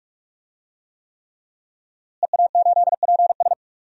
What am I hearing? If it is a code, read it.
Morse code, heard as a keyed tone, about 45 wpm, 700 Hz. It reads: EA8PS
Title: DXCC-100 Call Signs - 5 Characters - Repeated 3x - 45wpm